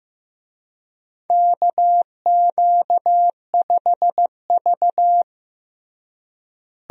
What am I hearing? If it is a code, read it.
Morse code, heard as a keyed tone, about 15 wpm, 700 Hz. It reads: KQ5V
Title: USA Call Signs - 4 Characters - Repeated 2x - 15wpm